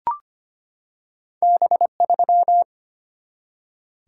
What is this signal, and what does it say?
Morse code, heard as a keyed tone, about 25 wpm, 690 Hz. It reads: B3
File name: Two Letter-Number Combinations - 25wpm